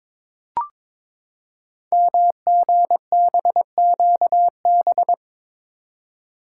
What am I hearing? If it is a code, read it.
Morse code, heard as a keyed tone, about 22 wpm, 700 Hz. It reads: MGBQB
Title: Code Groups - 5 Letters - 22wpm